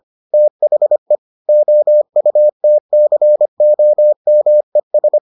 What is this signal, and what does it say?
Morse code, heard as a keyed tone, about 25 wpm, 600 Hz. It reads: THE OUTCOMES